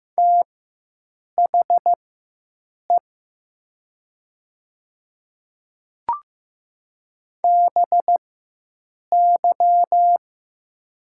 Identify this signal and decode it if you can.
Morse code, heard as a keyed tone, about 15 wpm, spaced as for 7 wpm, 700 Hz. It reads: THE BY
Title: 4x Character Spacing - Top 100 Words - 15wpm